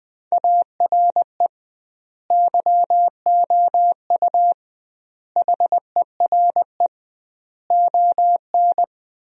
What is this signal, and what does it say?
Morse code, heard as a keyed tone, about 21 wpm, 700 Hz. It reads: ARE YOU HERE ON
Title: 2x Word-Spacing - Sentences from Top 400 Words - 20wpm